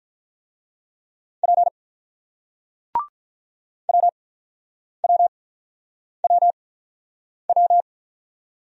Morse code, heard as a keyed tone, about 48 wpm, spaced as for 9 wpm, 700 Hz. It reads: P WWWW